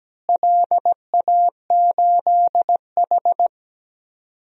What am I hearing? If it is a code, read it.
Morse code, heard as a keyed tone, about 17 wpm, 700 Hz. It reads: LA8H
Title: DXCC-100 Call Signs - 4 Characters - Repeated 2x - 17wpm